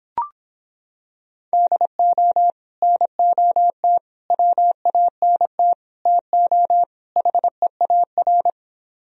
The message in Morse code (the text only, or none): DO NOT WANT TO HEAR